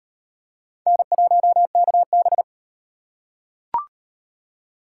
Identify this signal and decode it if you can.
Morse code, heard as a keyed tone, about 35 wpm, 700 Hz. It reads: N1KB